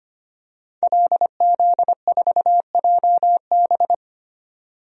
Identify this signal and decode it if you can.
Morse code, heard as a keyed tone, about 25 wpm, 700 Hz. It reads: LZ4JB
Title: DXCC-100 Call Signs - 5 Characters - 25wpm